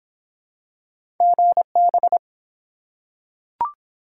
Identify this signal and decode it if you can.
Morse code, heard as a keyed tone, about 24 wpm, 700 Hz. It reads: GB